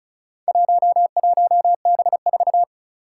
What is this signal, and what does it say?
Morse code, heard as a keyed tone, about 36 wpm, 700 Hz. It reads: 11B4
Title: Four Challenging Letter-Number Combinations 35wpm